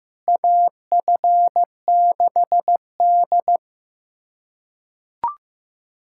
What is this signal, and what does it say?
Morse code, heard as a keyed tone, about 16 wpm, 700 Hz. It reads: AF6D
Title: USA Callsign DE Callsign - 4 Characters - 15wpm